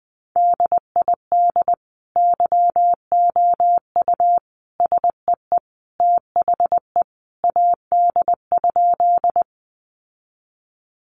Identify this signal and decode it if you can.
Morse code, heard as a keyed tone, about 20 wpm, 700 Hz. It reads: DID YOU SEE THE AD?
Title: Sentences from Top 1300 Words 20wpm